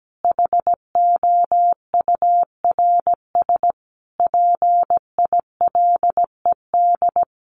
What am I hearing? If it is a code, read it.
Morse code, heard as a keyed tone, about 17 wpm, 700 Hz. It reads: HOURS PILED
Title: A Sound Of Thunder 17wpm